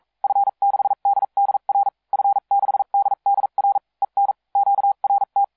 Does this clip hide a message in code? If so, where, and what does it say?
Yes: Morse code from 0.2 to 5.4 s, heard as a keyed tone, about 42 wpm, spaced as for 36 wpm, 800 Hz. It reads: F6DDR F6DDR EN QRT